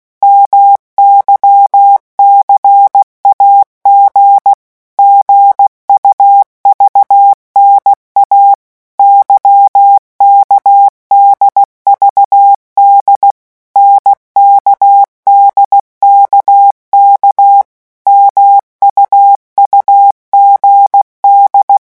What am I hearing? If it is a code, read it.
Morse code, heard as a keyed tone, about 16 wpm, 800 Hz. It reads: MYCAG GUVNA YKDVD NKDKK MUUGD
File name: morse mp3 / revision3-vitesse 15 mots